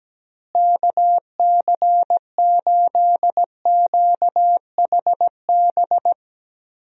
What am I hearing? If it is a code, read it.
Morse code, heard as a keyed tone, about 17 wpm, 700 Hz. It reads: KC8QHB
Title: USA Call Signs - 6 Characters - Repeated 3x - 17wpm